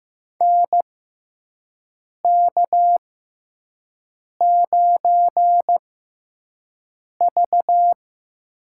Morse code, heard as a keyed tone, about 15 wpm, spaced as for 5 wpm, 700 Hz. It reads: NK9V
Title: USA Call Signs - 4 Characters - 6x Character Spacing - 15wpm